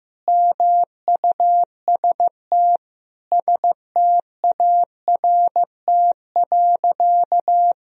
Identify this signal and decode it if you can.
Morse code, heard as a keyed tone, about 16 wpm, 700 Hz. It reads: MUST START.
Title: Sentences from Top 1000 Words - Review Part 1 - 15wpm